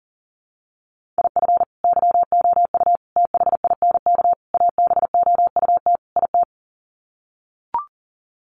Morse code, heard as a keyed tone, about 40 wpm, 700 Hz. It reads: IF YOU THINK ABOUT IT
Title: Sets of 5 Words Encore2 40wpm